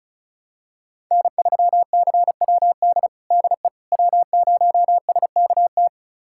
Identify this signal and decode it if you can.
Morse code, heard as a keyed tone, about 36 wpm, 700 Hz. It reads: N3CWD DE W0SKT
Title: USA Callsign DE Callsign - 5 Characters - 35wpm